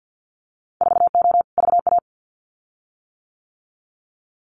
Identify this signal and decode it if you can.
Morse code, heard as a keyed tone, about 50 wpm, 700 Hz. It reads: 4O VA
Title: Winter Field Day Exchanges Encore 50wpm